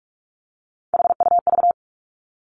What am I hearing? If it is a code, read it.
Morse code, heard as a keyed tone, about 46 wpm, 700 Hz. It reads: HUV
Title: Three Challenging Letter-Number Combinations 45wpm